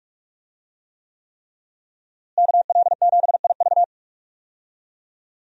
Morse code, heard as a keyed tone, about 41 wpm, 700 Hz. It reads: KL7IV
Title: USA Call Signs - 5 Characters - Repeated 3x - 45wpm